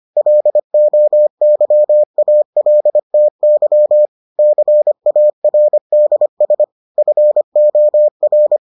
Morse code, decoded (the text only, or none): LOYALTY CARDS FOR